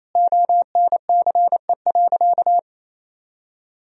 Morse code, heard as a keyed tone, about 28 wpm, 700 Hz. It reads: ONCE.
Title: 4x Word-Spacing - Sentences from Top 500 Words - 28wpm